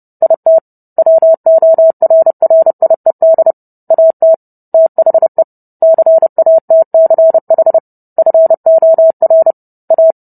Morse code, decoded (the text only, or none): IT WORRIED AT THE CATCH FOR A